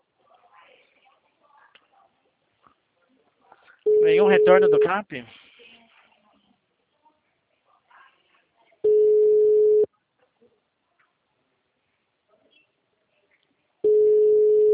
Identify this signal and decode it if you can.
Ringback tone (Europe, 425 Hz), with speech.
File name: out- -2007- - - .41130